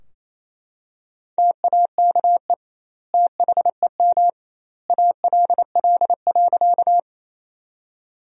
Morse code, heard as a keyed tone, about 28 wpm, 700 Hz. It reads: TAKE THEM ALL.
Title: 2x Word-Spacing - Sentences from Top 200 Words - 28wpm